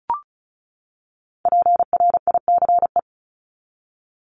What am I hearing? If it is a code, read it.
Morse code, heard as a keyed tone, about 34 wpm, 700 Hz. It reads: PRICE
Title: Top 501-750 Words 35wpm